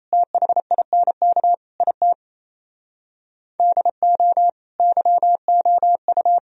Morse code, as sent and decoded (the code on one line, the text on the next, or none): - .... .. -. -.- / .. - / -.. --- / -.-- --- ..-
THINK IT DO YOU